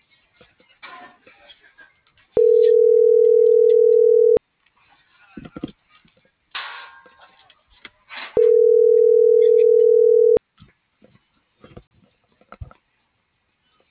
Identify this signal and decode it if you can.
Ringback tone (North America), 440 + 480 Hz.